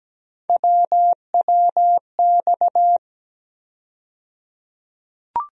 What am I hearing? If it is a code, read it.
Morse code, heard as a keyed tone, about 17 wpm, 700 Hz. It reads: WWX
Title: USA Call Sign Postfixes - 3 Letters 17wpm